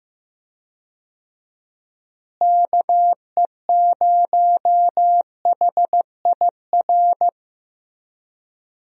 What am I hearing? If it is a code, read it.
Morse code, heard as a keyed tone, about 15 wpm, 700 Hz. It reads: KE0HIR
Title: USA Call Signs - 6 Characters - Repeated 3x - 15wpm